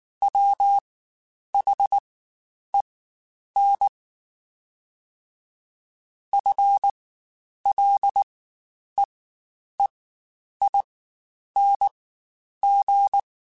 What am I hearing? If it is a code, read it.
Morse code, heard as a keyed tone, about 19 wpm, spaced as for 9 wpm, 800 Hz. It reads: WHEN FLEEING
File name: Spaced / Chuckles Spaced-mp3s / Temptation sp @18